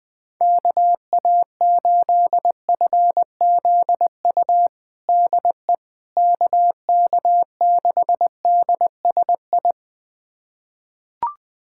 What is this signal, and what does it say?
Morse code, heard as a keyed tone, about 20 wpm, 700 Hz. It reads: KA8FZU DE KK6DSI
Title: USA Callsign DE Callsign - 6 Characters - 20wpm